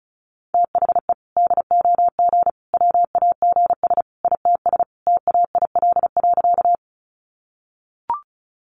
Morse code, heard as a keyed tone, about 34 wpm, 700 Hz. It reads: THE DOG WAGS ITS TAIL.
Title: Easy Phrases 35wpm